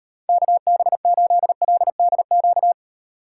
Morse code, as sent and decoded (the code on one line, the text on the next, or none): -.- -... ---.. .-.. -.. --.-
KB8LDQ